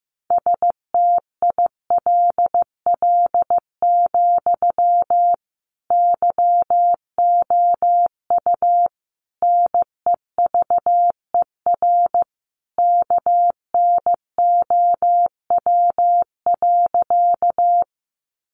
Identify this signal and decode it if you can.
Morse code, heard as a keyed tone, about 15 wpm, 700 Hz. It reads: STILL, YOU NEVER KNOW.
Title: Sentences from Top 300 Words 15wpm